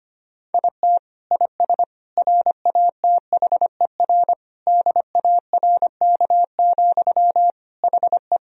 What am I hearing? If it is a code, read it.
Morse code, heard as a keyed tone, about 25 wpm, 700 Hz. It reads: IT IS RATHER DARK, HE
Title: Sentences from Top 1000 Words - Review Part 4 - 25wpm